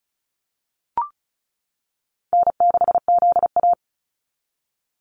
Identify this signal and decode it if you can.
Morse code, heard as a keyed tone, about 34 wpm, 700 Hz. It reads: N6ZA
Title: NA SOTA Activator Callsigns 35wpm